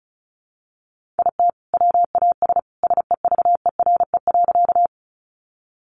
Morse code, heard as a keyed tone, about 34 wpm, 700 Hz. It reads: IT WAS SEVERE.
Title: Sentences from Top 1400 Words 35wpm